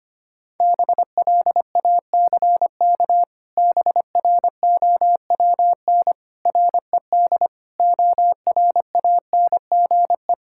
Morse code, heard as a keyed tone, about 25 wpm, 700 Hz. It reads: BLACK BROWN RED ORANGE
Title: Pangram Challenge v2 - 25wpm